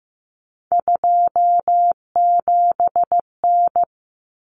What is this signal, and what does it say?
Morse code, heard as a keyed tone, about 15 wpm, 700 Hz. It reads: 27N
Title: Common Cut Numbers - 3-Digit 15wpm